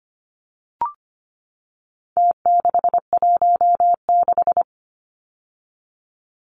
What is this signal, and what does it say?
Morse code, heard as a keyed tone, about 25 wpm, 700 Hz. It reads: T616
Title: Numbers - Contest Style - 4-Digit 25wpm